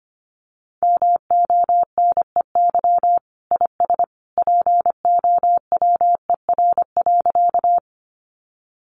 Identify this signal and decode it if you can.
Morse code, heard as a keyed tone, about 24 wpm, 700 Hz. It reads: MONEY IS POWER.
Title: Sentences from Top 500 Words 25wpm